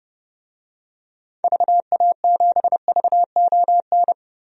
Morse code, heard as a keyed tone, about 30 wpm, 700 Hz. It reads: VA7VON
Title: DXCC-100 Call Signs - 6 Characters - 30wpm